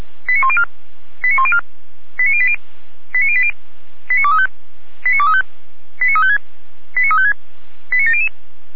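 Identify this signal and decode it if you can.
Five-tone selective call (ZVEI), heard as quick runs of five tones, tones of about 70 ms. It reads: 89104 89104 89080 89080 89235 89235 89356 89356 89700
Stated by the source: ZVEI five-tone selective call received on SDR.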